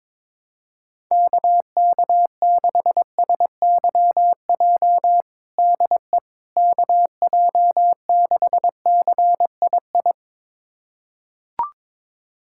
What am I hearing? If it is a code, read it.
Morse code, heard as a keyed tone, about 22 wpm, 700 Hz. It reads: KK6SYJ DE KJ6CII
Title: USA Callsign DE Callsign - 6 Characters - 22wpm